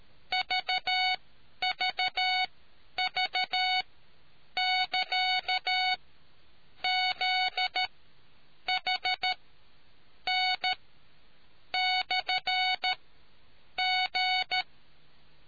Morse code, heard as a keyed tone, about 13 wpm, spaced as for 8 wpm, 740 Hz. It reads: VVV<KA>ZHN/G